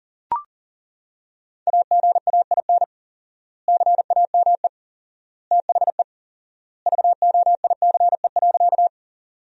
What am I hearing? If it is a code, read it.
Morse code, heard as a keyed tone, about 40 wpm, 700 Hz. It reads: AGAIN CAME THE VOICE.
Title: 4x Word-Spacing - Sentences from Top 500 Words - 40wpm